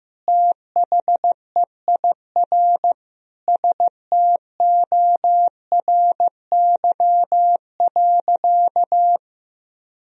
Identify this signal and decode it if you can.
Morse code, heard as a keyed tone, about 15 wpm, 700 Hz. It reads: THEIR STORY.
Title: Sentences from Top 1000 Words - Review Part 1 - 15wpm